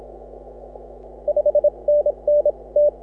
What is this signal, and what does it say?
Morse code, heard as a keyed tone, about 36 wpm, spaced as for 27 wpm, 530 Hz. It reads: 5NN T
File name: AUDIO2018 / HA6OI CW